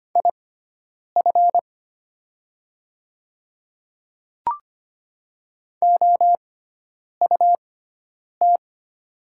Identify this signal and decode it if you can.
Morse code, heard as a keyed tone, about 25 wpm, spaced as for 9 wpm, 700 Hz. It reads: IF OUT
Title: 6x Character Spacing - Top 100 Words - 25wpm